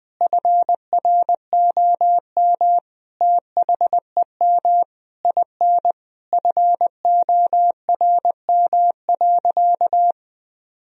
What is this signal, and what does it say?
Morse code, heard as a keyed tone, about 20 wpm, 700 Hz. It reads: FROM THEM IN FORM.
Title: Sentences from Top 1000 Words - Review Part 1 - 20wpm